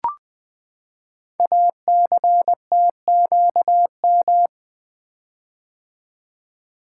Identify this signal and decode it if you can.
Morse code, heard as a keyed tone, about 20 wpm, 700 Hz. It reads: ACTQM
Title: Code Groups - 5 Letters - 20wpm